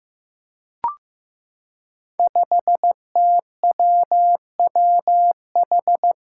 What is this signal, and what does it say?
Morse code, heard as a keyed tone, about 15 wpm, 700 Hz. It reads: 5TWWH
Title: Five Challenging Letter-Number Combinations 15wpm